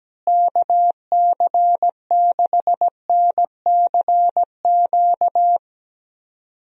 Morse code, heard as a keyed tone, about 17 wpm, 700 Hz. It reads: KC6NCQ
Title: USA Call Signs - 6 Characters - Repeated 2x - 17wpm